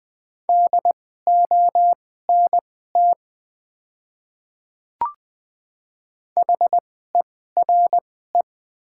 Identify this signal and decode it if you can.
Morse code, heard as a keyed tone, about 20 wpm, spaced as for 14 wpm, 690 Hz. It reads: DONT HERE
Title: 2x Character Spacing - Top 500 Words - 20wpm